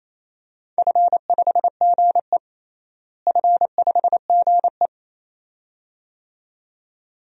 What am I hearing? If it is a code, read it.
Morse code, heard as a keyed tone, about 28 wpm, 700 Hz. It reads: F5GE F5GE
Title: DXCC-100 Call Signs - 4 Characters - Repeated 2x - 28wpm